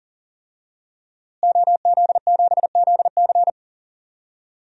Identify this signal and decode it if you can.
Morse code, heard as a keyed tone, about 37 wpm, 700 Hz. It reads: OZ7ZC
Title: DXCC-100 Call Signs - 5 Characters - Repeated 2x - 40wpm